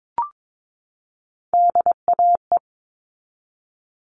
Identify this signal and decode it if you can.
Morse code, heard as a keyed tone, about 22 wpm, 700 Hz. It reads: DAE
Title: All Cut Numbers - 3-Digit 22wpm